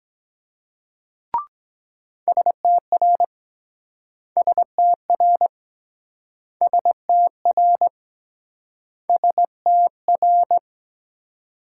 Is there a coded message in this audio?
Yes: Morse code from 2.3 to 10.6 s, heard as a keyed tone, about 22 wpm, 700 Hz. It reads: STR STR STR STR